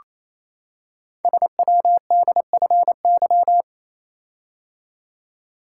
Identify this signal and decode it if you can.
Morse code, heard as a keyed tone, about 29 wpm, 700 Hz. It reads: SWDFY
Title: Code Groups - 5 Letters - 28wpm